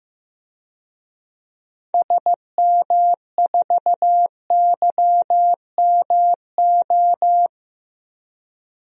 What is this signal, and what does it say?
Morse code, heard as a keyed tone, about 15 wpm, 700 Hz. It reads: SM4YMO
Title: DXCC-100 Call Signs - 6 Characters - 15wpm